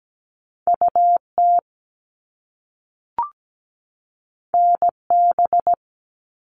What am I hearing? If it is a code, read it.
Morse code, heard as a keyed tone, about 17 wpm, 700 Hz. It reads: UT NB
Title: All Cut Numbers - 2-Digit 17wpm